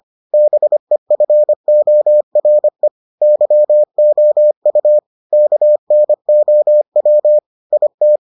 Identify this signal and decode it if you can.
Morse code, heard as a keyed tone, about 25 wpm, 600 Hz. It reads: BEFORE YOU KNOW IT